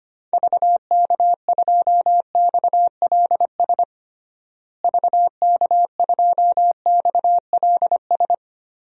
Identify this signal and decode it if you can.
Morse code, heard as a keyed tone, about 25 wpm, 700 Hz. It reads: VK2XLS VK2XLS
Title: DXCC-100 Call Signs - 6 Characters - Repeated 3x - 25wpm